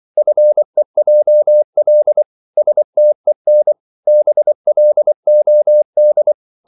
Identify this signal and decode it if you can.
Morse code, heard as a keyed tone, about 24 wpm, 600 Hz. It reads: FEJL STEN BLOD
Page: Korte danske ord 24wpm